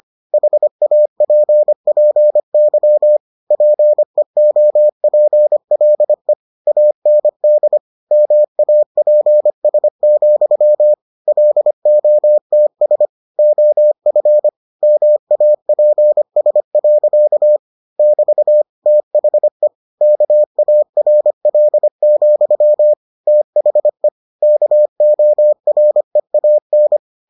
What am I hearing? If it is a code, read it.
Morse code, heard as a keyed tone, about 25 wpm, 600 Hz. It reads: HAPPY PEOPLE AND MAPS, LOTS OF MAPS. <BT> THE KARL, THE KOREAN